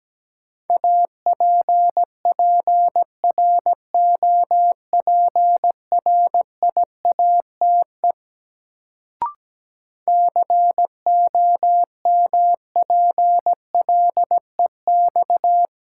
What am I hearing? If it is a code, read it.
Morse code, heard as a keyed tone, about 17 wpm, 700 Hz. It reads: APPROPRIATE COMPLEX
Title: Sentences from Top 900 Words - New Word Drill - 17wpm